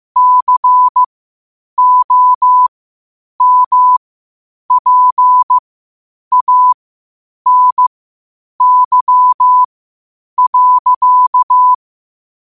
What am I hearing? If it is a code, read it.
Morse code, heard as a keyed tone, about 15 wpm, spaced as for 8 wpm, 1000 Hz. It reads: COMPANY.
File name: 8wpm